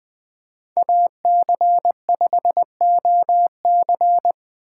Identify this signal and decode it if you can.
Morse code, heard as a keyed tone, about 20 wpm, 700 Hz. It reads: AC5OC
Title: 2023 POTA Hunter Call Signs - 20wpm